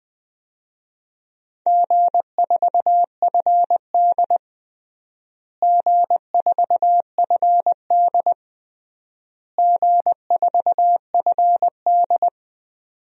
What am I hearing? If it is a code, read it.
Morse code, heard as a keyed tone, about 20 wpm, 700 Hz. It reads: G4FD G4FD G4FD
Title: DXCC-100 Call Signs - 4 Characters - Repeated 3x - 20wpm